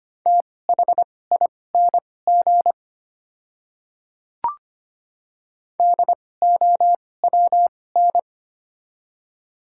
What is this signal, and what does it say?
Morse code, heard as a keyed tone, about 25 wpm, spaced as for 18 wpm, 700 Hz. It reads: THING DOWN
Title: 2x Character Spacing - Top 100 Words - 25wpm